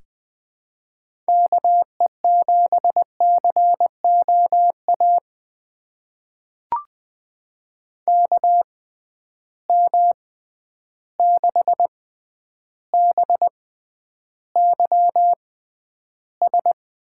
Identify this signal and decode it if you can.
Morse code, heard as a keyed tone, about 20 wpm, spaced as for 10 wpm, 700 Hz. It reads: KE7COA KM6BYS
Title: USA Call Signs - 6 Characters - 6x Character Spacing - 20wpm